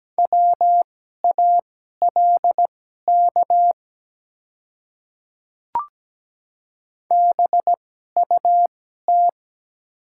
Morse code, heard as a keyed tone, about 17 wpm, spaced as for 12 wpm, 700 Hz. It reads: WALK BUT
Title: 2x Character Spacing - Top 300 Words - 17wpm